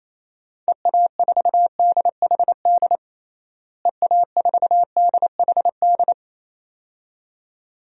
Morse code, heard as a keyed tone, about 28 wpm, 700 Hz. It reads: EA4DHD EA4DHD
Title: DXCC-100 Call Signs - 6 Characters - Repeated 2x - 28wpm